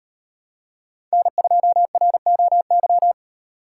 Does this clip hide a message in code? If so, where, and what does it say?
Yes: Morse code from 1.1 to 3.1 s, heard as a keyed tone, about 38 wpm, 700 Hz. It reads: N2ROY